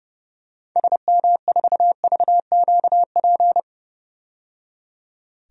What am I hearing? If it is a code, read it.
Morse code, heard as a keyed tone, about 32 wpm, 700 Hz. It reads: SM4VQP
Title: DXCC-100 Call Signs - 6 Characters - 30wpm